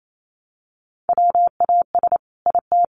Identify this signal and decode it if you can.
Morse code, heard as a keyed tone, about 28 wpm, 700 Hz. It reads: WAS IT
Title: A Sound Of Thunder 28wpm